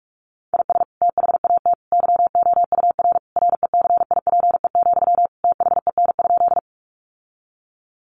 Morse code, heard as a keyed tone, about 44 wpm, 700 Hz. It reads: IS THAT YOUR RECIPE, THEN?
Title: Sentences from Top 1200 Words 45wpm